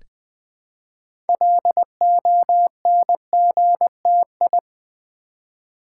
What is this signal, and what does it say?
Morse code, heard as a keyed tone, about 20 wpm, 700 Hz. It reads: LONGTI
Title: Build A Word - 3 Letters At A Time - 8-Letter Words - 20wpm